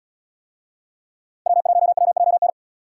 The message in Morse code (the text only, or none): W1WJN